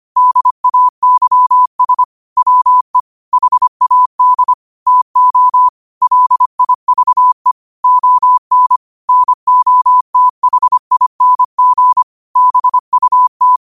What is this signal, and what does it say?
Morse code, heard as a keyed tone, about 25 wpm, 1000 Hz. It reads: DAYS WE HAD TO LIVE ON NOTHING BUT
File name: qotdfiles / 25wpm